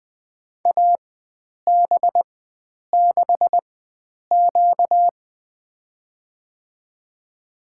Morse code, heard as a keyed tone, about 20 wpm, spaced as for 9 wpm, 700 Hz. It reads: AB6Q
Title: USA Call Signs - 4x Character Spacing - 20wpm